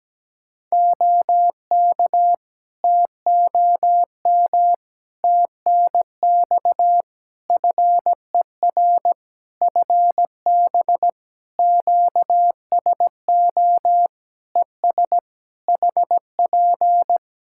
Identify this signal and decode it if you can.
Morse code, heard as a keyed tone, about 17 wpm, 700 Hz. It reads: OK TOM TNX FER FB QSO ES HP
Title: Practice QSO - Exchange 4 - Ending - 17wpm